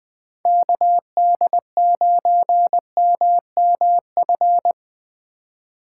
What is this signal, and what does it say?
Morse code, heard as a keyed tone, about 24 wpm, 700 Hz. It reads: KD9MMF
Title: Speed Racing - USA 2x3 Call Signs 20wpm